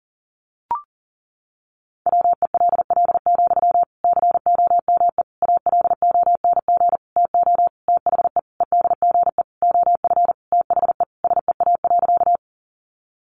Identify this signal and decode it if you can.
Morse code, heard as a keyed tone, about 40 wpm, 700 Hz. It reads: WELL, COME ALONG TO THE EDGE OF THE SEA.
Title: Sentences from Top 1300 Words 40wpm